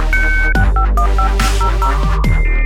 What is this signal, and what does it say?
DTMF, heard as telephone touch tones, tones of about 100 ms. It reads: 651500